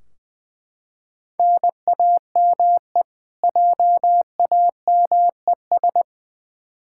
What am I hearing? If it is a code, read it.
Morse code, heard as a keyed tone, about 21 wpm, 700 Hz. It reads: NAME JAMES
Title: Practice QSO - Exchange 1 - Beginning - 20wpm